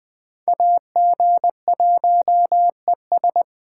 Morse code, heard as a keyed tone, about 24 wpm, 700 Hz. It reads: AG1ES
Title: Speed Racing - USA 2x2 Call Signs 20wpm